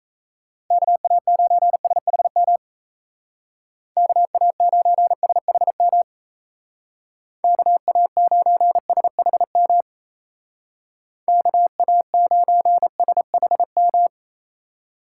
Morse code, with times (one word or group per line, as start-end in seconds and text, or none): 0.7-2.6 KA9SHM
4.0-6.0 KA9SHM
7.4-9.8 KA9SHM
11.3-14.1 KA9SHM